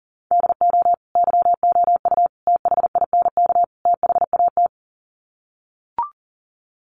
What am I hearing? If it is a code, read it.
Morse code, heard as a keyed tone, about 40 wpm, 700 Hz. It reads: DO YOU THINK THAT
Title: Sets of 4 Words 40wpm